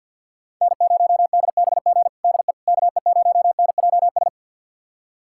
Morse code, heard as a keyed tone, about 52 wpm, 700 Hz. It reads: N0DBG DE KE0NJS
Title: USA Callsign DE Callsign 50wpm